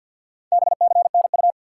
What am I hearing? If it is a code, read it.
Morse code, heard as a keyed tone, about 53 wpm, 700 Hz. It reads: BCNU
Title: Top 90 QSO Elements 50wpm